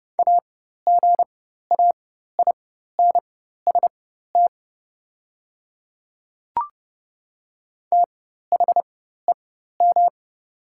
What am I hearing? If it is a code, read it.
Morse code, heard as a keyed tone, about 30 wpm, spaced as for 14 wpm, 690 Hz. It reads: AGAINST THEM
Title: 4x Character Spacing - Top 500 Words - 30wpm